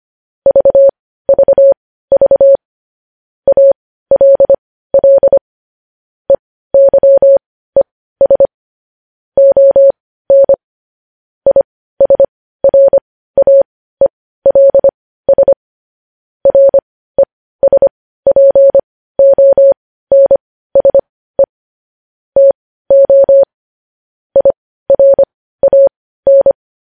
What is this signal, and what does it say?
Morse code, heard as a keyed tone, about 25 wpm, spaced as for 15 wpm, 550 Hz. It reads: VVV ALL EYES ON ISRAELS RESPONSE TO IRAN
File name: podcast / News headlines-1x-25x15